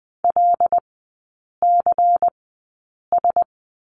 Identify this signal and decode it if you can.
Morse code, heard as a keyed tone, about 20 wpm, spaced as for 8 wpm, 690 Hz. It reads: LCS
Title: 093 Lesson 16e - Letter c - Three Characters 20wpm